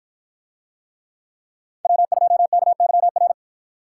Morse code, heard as a keyed tone, about 53 wpm, 700 Hz. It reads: W2KYR